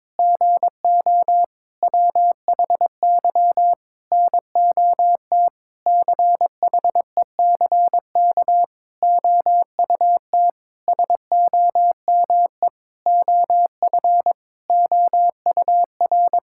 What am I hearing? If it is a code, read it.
Morse code, heard as a keyed tone, about 22 wpm, 700 Hz. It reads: GO WHY NOT CHECK OUT SOME OF OUR
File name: Sentences with 9-Word Phrases - 22wpm